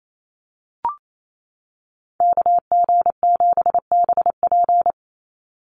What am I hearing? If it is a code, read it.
Morse code, heard as a keyed tone, about 28 wpm, 700 Hz. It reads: KG7BP
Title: USA Call Signs 28wpm